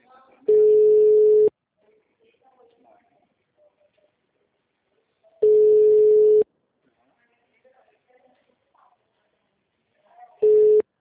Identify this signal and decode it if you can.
Ringback tone (Europe, 425 Hz), with speech.
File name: out- -2012- - - .39643